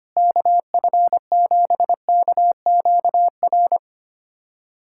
Morse code, heard as a keyed tone, about 25 wpm, 700 Hz. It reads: KF7KQR
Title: USA Callsign DE Callsign 25wpm